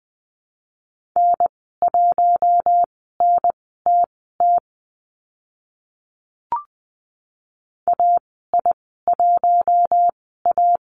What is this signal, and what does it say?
Morse code, heard as a keyed tone, about 20 wpm, spaced as for 15 wpm, 700 Hz. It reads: N1NTT AI1A
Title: 042 Lesson 8g - Number 1 - Call Signs 20wpm